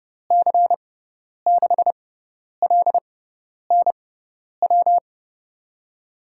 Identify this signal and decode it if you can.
Morse code, heard as a keyed tone, about 30 wpm, spaced as for 10 wpm, 700 Hz. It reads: C6LNW
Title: USA Call Signs - 6 Characters - 6x Character Spacing - 30wpm